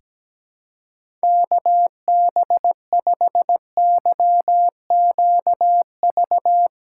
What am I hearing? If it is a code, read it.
Morse code, heard as a keyed tone, about 17 wpm, 700 Hz. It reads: KB5YQV
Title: USA Callsign DE Callsign 17wpm